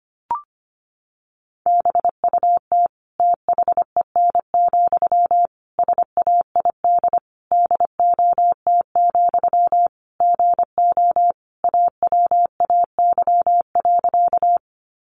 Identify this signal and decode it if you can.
Morse code, heard as a keyed tone, about 25 wpm, 700 Hz. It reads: BUT THEN, SAID DOT, GO AWAY.
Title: Sentences from Top 1500 Words 25wpm